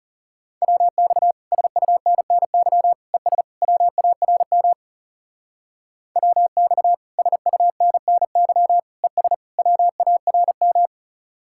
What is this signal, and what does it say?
Morse code, heard as a keyed tone, about 36 wpm, 700 Hz. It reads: WX SUNNY ES WARM WX SUNNY ES WARM
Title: Speed-Racing Common QSO Phrases - 30wpm